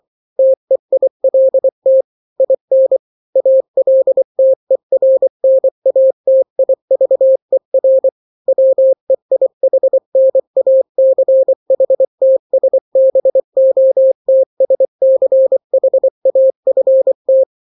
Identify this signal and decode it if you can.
Morse code, heard as a keyed tone, about 24 wpm, 530 Hz. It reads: TEILT IN ALTERNATIVER WEIHNACHTSBOTSCHAFT
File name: daily / 23wpm